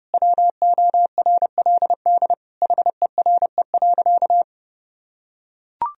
Morse code, heard as a keyed tone, about 30 wpm, 700 Hz. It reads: WORLD HERE.
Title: Sentences from Top 1000 Words - Review Part 3 - 30wpm